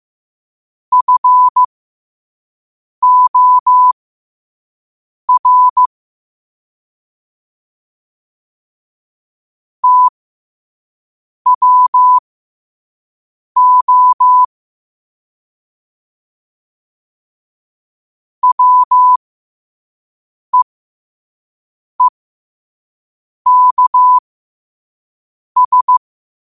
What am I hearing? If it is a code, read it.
Morse code, heard as a keyed tone, about 15 wpm, spaced as for 5 wpm, 1000 Hz. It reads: FOR TWO WEEKS
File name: qotdfiles / 5wpm